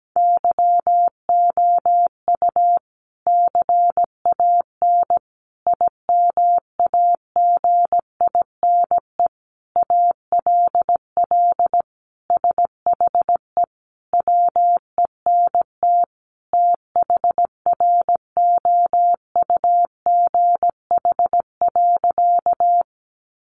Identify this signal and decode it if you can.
Morse code, heard as a keyed tone, about 17 wpm, 700 Hz. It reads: YOU CAN IMAGINE ALL SHE WENT THROUGH.
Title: Sentences from Top 1400 Words 17wpm